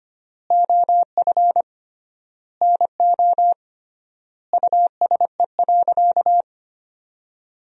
Morse code, heard as a keyed tone, about 25 wpm, 700 Hz. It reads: OF NO USE.
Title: 3x Word-Spacing - Sentences from Top 100 Words - 25wpm